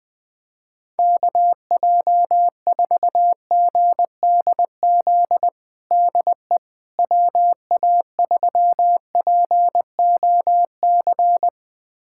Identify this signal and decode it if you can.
Morse code, heard as a keyed tone, about 20 wpm, 700 Hz. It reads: KJ4GDZ DE WA3POC
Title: USA Callsign DE Callsign 20wpm